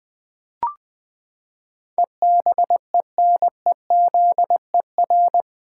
Morse code, heard as a keyed tone, about 21 wpm, 700 Hz. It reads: EBENEZER
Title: Pangram Challenge v2 - 20wpm